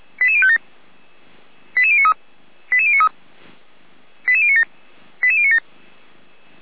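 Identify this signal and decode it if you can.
Five-tone selective call (ZVEI), heard as quick runs of five tones, tones of about 70 ms. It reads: 88057 88093 88093 88097 88097
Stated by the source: ZVEI five-tone selective call received on SDR.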